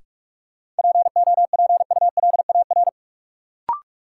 Morse code, heard as a keyed tone, about 46 wpm, 700 Hz. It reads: POPULAR
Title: Sentences from Top 1200 Words - New Word Drill Alt - 45wpm